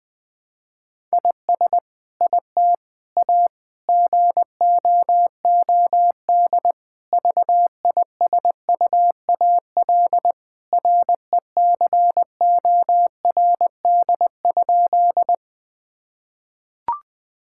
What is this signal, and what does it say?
Morse code, heard as a keyed tone, about 21 wpm, 700 Hz. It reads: IS IT A GOOD VISUAL RECORD?
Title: Sentences from Top 1000 Words - Review Part 6 - 20wpm